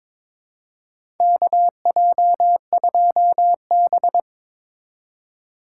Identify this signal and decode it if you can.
Morse code, heard as a keyed tone, about 21 wpm, 700 Hz. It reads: KJ2B
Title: USA Call Signs - 3x Character Spacing - 22wpm